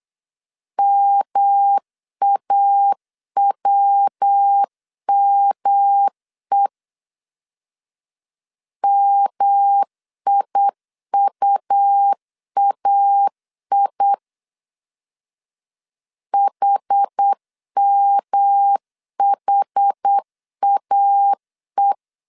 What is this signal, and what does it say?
Morse code, heard as a keyed tone, about 8 wpm, 800 Hz. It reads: MAWME MIUAI HMHAE
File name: CW / DET / 1 3 1500C S B 8wpm 4141